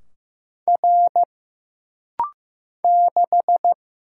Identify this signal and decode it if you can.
Morse code, heard as a keyed tone, about 15 wpm, 700 Hz. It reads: R 6
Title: Single Letter-Number Alt - Rapid-Fire 15wpm